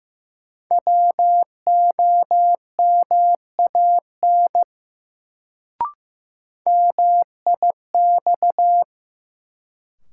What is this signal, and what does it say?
Morse code, heard as a keyed tone, about 15 wpm, 700 Hz. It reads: WOMAN MIX
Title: Sentences from Top 1100 Words - New Word Drill Alt - 15wpm